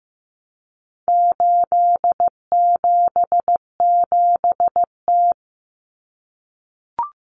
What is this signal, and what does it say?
Morse code, heard as a keyed tone, about 14 wpm, 700 Hz. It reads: 877T
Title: Common Cut Numbers - 4-Digit 15wpm